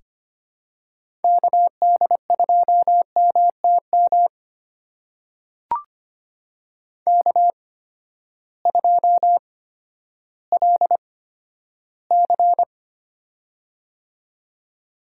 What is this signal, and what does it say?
Morse code, heard as a keyed tone, about 25 wpm, spaced as for 12 wpm, 700 Hz. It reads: KD2MTM K2LC
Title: USA Call Signs - 8x Character Spacing - 25wpm